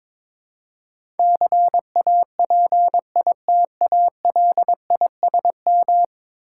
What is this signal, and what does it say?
Morse code, heard as a keyed tone, about 23 wpm, 700 Hz. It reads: CAPITALISM
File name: practice / 22wpm / 10 Letter Words - First 3 Letters Given - 22wpm